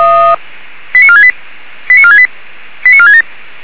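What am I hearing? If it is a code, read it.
Five-tone selective call (ZVEI), heard as quick runs of five tones, tones of about 70 ms. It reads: 89468 89468 89467
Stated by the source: ZVEI five-tone call received by SDR, recorded by fire brigade Feuerwehr Orlamünde.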